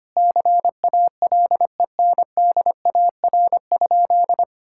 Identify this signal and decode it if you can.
Morse code, heard as a keyed tone, about 25 wpm, 700 Hz. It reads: CALENDAR?
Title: Sentences from Top 1600 Words 25wpm